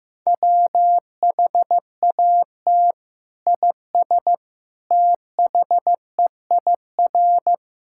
Morse code, heard as a keyed tone, about 15 wpm, 700 Hz. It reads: WHAT IS THEIR
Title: Sentences from Top 1000 Words - Review Part 5 - 15wpm